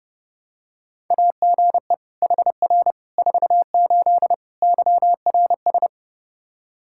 Morse code, heard as a keyed tone, about 30 wpm, 700 Hz. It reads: AGE HR 48 YRS
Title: K1USN SST QSO Practice 30wpm